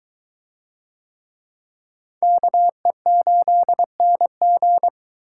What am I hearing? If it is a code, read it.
Morse code, heard as a keyed tone, about 22 wpm, 700 Hz. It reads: KE8NG